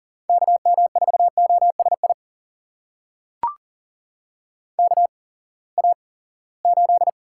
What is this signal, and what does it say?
Morse code, heard as a keyed tone, about 40 wpm, spaced as for 23 wpm, 700 Hz. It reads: KM4OSI KA8
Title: USA Call Signs - 6 Characters - 8x Character Spacing - 40wpm